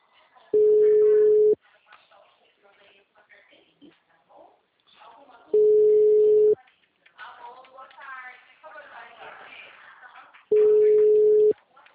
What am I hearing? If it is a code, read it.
Ringback tone (Europe, 425 Hz), with speech.